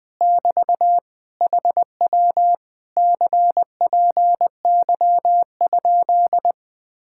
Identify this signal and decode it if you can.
Morse code, heard as a keyed tone, about 20 wpm, 700 Hz. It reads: <BT> HW CPY?
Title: Speed-Racing Common QSO Phrases Encore - 17wpm